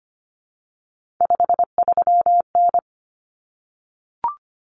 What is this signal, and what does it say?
Morse code, heard as a keyed tone, about 25 wpm, 700 Hz. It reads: 53N
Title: Common Cut Numbers - 3-Digit 25wpm